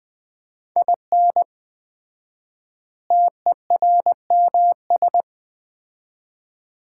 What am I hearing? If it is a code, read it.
Morse code, heard as a keyed tone, about 21 wpm, 700 Hz. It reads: IN TERMS
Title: 4x Word-Spacing - Sets of 4 Words - 20wpm